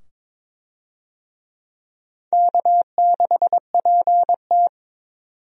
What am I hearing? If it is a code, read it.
Morse code, heard as a keyed tone, about 22 wpm, 700 Hz. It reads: K6PT
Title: USA Call Signs - 4 Characters - 22wpm